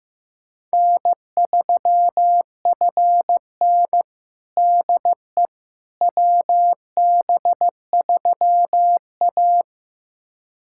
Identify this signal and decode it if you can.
Morse code, heard as a keyed tone, about 16 wpm, 700 Hz. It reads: N3FN DE WB3A
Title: USA Callsign DE Callsign - 4 Characters - 15wpm